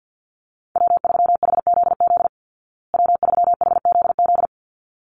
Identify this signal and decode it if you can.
Morse code, heard as a keyed tone, about 48 wpm, 700 Hz. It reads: W3HZZ W3HZZ
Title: Call Signs Repeated 3x - 50wpm